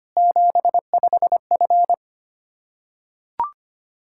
Morse code, heard as a keyed tone, about 25 wpm, 700 Hz. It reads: 75F
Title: Practice QSO - Exchange 2 - Info - 25wpm